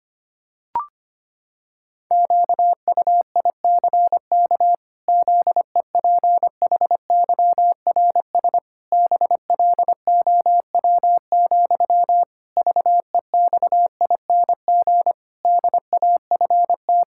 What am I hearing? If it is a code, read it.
Morse code, heard as a keyed tone, about 25 wpm, 700 Hz. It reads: QUICK ZEPHYRS BLOW, VEXING DAFT
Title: Pangram Challenge v2 - 25wpm